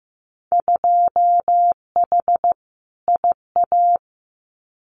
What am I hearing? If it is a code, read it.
Morse code, heard as a keyed tone, about 15 wpm, 700 Hz. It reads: 2H IA
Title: Winter Field Day Exchanges Encore 15wpm